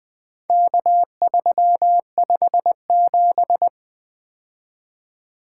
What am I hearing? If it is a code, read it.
Morse code, heard as a keyed tone, about 20 wpm, 700 Hz. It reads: K357
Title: Four Letter-Number Combinations - 20wpm